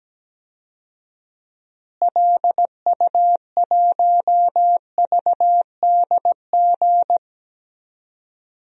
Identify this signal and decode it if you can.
Morse code, heard as a keyed tone, about 17 wpm, 700 Hz. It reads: LU1VDG
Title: DXCC-100 Call Signs - 6 Characters - Repeated 2x - 17wpm